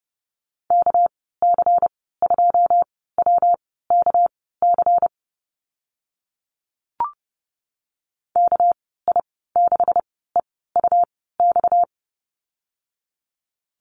Morse code, heard as a keyed tone, about 30 wpm, spaced as for 17 wpm, 700 Hz. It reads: KC2WKC KI6EUX
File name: Lesson 37g - Letter x - Call Signs 30wpm